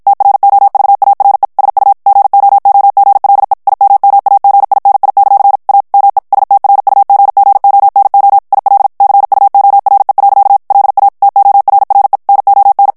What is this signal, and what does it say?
Morse code, heard as a keyed tone, about 53 wpm, 800 Hz. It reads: TROVARE SU GOOGLE IMMAGINI; A ME STRUGGONO IL CUORE. LA TORRE NON
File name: ITALIEN texte clair 1 vitesse 50 mots minute